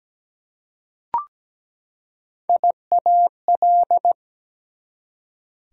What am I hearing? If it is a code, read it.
Morse code, heard as a keyed tone, about 17 wpm, 700 Hz. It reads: IAL
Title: Prefixes and Suffixes 17wpm